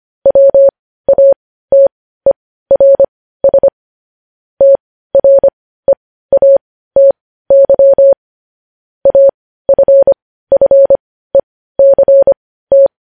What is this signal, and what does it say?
Morse code, heard as a keyed tone, about 25 wpm, spaced as for 15 wpm, 550 Hz. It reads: WATERS TREATY AFFECT
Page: news-headlines-in-morse-code-at-15-wpm